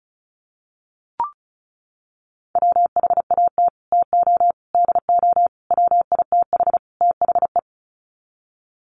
Morse code, heard as a keyed tone, about 34 wpm, 700 Hz. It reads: WHAT TO DO WITH THE
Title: Sets of 5 Words Encore2 35wpm